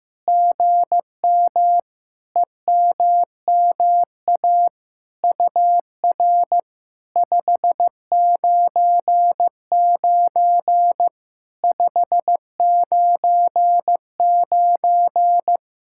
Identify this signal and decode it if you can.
Morse code, heard as a keyed tone, about 15 wpm, 700 Hz. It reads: GM EMMA UR 599 599
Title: SOTA QSO Practice 15wpm